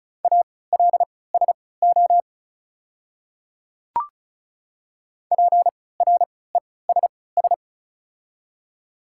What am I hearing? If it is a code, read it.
Morse code, heard as a keyed tone, about 36 wpm, spaced as for 20 wpm, 700 Hz. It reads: ALSO PRESS
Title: 3x Character Spacing - Top 500 Words - 35wpm